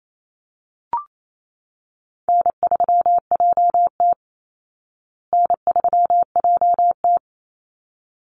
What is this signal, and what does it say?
Morse code, heard as a keyed tone, about 28 wpm, 700 Hz. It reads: N3JT N3JT
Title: Easy Call Signs Repeated 2x - 28wpm